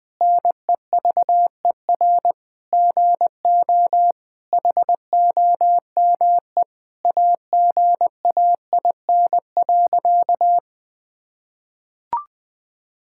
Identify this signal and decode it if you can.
Morse code, heard as a keyed tone, about 20 wpm, 700 Hz. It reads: NEVER GO HOME AGAIN.
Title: Sentences from Top 1000 Words - Review Part 6 - 20wpm